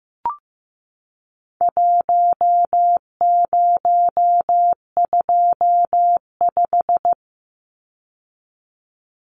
Morse code, heard as a keyed tone, about 15 wpm, 700 Hz. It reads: 1025
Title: Numbers - Contest Style - 4-Digit 15wpm